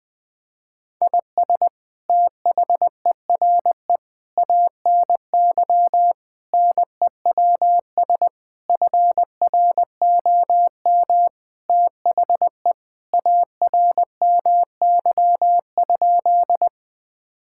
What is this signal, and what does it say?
Morse code, heard as a keyed tone, about 20 wpm, 700 Hz. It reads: IS THERE ANY NEWS FROM THE ARMY?
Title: Sentences from Top 1000 Words - Review Part 5 - 20wpm